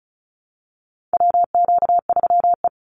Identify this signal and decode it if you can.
Morse code, heard as a keyed tone, about 34 wpm, 700 Hz. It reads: WQ3E
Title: Call Signs Repeated 2x - 35wpm